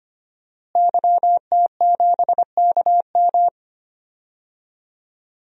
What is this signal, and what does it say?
Morse code, heard as a keyed tone, about 25 wpm, 700 Hz. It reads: YT7KM
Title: DXCC-100 Call Signs - 5 Characters - Repeated 2x - 25wpm